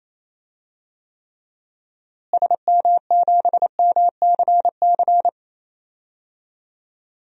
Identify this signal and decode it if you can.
Morse code, heard as a keyed tone, about 28 wpm, 700 Hz. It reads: SM7MCC
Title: DXCC-100 Call Signs - 6 Characters - Repeated 2x - 28wpm